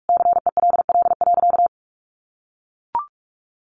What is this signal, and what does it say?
Morse code, heard as a keyed tone, about 44 wpm, 700 Hz. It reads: CELL.
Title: Sentences from Top 1100 Words 45wpm